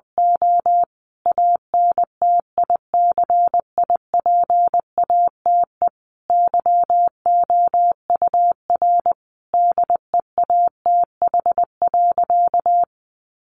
Morse code, heard as a keyed tone, about 20 wpm, 700 Hz. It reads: O ANTICIPATE YOUR DEATH.
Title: Sentences from Top 1500 Words 20wpm